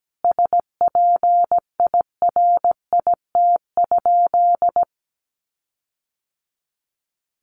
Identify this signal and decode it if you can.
Morse code, heard as a keyed tone, about 17 wpm, 700 Hz. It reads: SPIRIT?
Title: Sentences from Top 1400 Words 17wpm